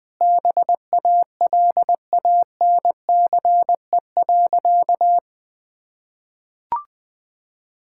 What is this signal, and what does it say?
Morse code, heard as a keyed tone, about 20 wpm, 700 Hz. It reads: BALANCE.
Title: Sentences from Top 1000 Words - Review Part 6 - 20wpm